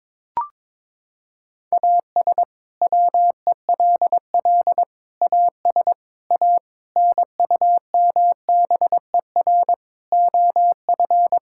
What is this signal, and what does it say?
Morse code, heard as a keyed tone, about 22 wpm, 700 Hz. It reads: AS WELL AS A NUMBER OF
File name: practice / 22wpm / Sets of 7 Words - 22wpm